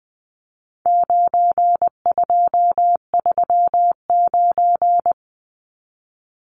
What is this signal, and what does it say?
Morse code, heard as a keyed tone, about 20 wpm, 700 Hz. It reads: 9239
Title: Numbers - 4-Digit 20wpm